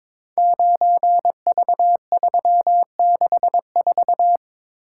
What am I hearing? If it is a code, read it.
Morse code, heard as a keyed tone, about 22 wpm, 700 Hz. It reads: 9V364
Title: Code Groups - 5 Letter:Numbers - 22wpm